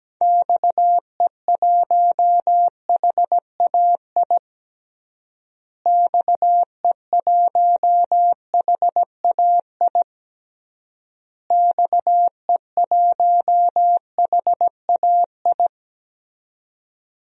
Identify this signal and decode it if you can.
Morse code, heard as a keyed tone, about 17 wpm, 700 Hz. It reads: XE1HAI XE1HAI XE1HAI
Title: DXCC-100 Call Signs - 6 Characters - Repeated 3x - 17wpm